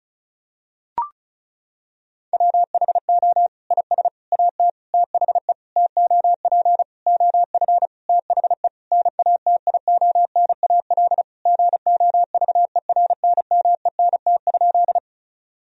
Morse code, heard as a keyed tone, about 36 wpm, 700 Hz. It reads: WHO IS AT THE TOP OF THE NATIONAL GOVERNMENT?
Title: Sentences from Top 1000 Words - Review Part 4 - 35wpm